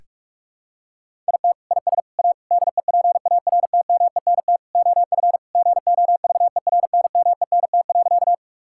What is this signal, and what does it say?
Morse code, heard as a keyed tone, about 41 wpm, 700 Hz. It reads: IT IS A DEPARTMENT OF GOVERNMENT.
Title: Sentences from Top 1000 Words - Review Part 5 - 45wpm